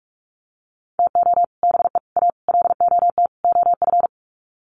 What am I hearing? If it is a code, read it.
Morse code, heard as a keyed tone, about 44 wpm, 700 Hz. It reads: TO BE A LOT OF
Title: Sets of 5 Words Encore 45wpm